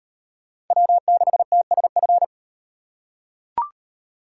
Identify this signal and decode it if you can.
Morse code, heard as a keyed tone, about 33 wpm, 700 Hz. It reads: W6TSF